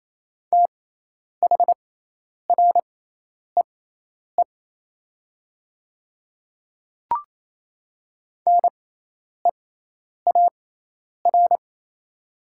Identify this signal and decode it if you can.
Morse code, heard as a keyed tone, about 28 wpm, spaced as for 10 wpm, 690 Hz. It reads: THREE NEAR
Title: 6x Character Spacing - Top 500 Words - 28wpm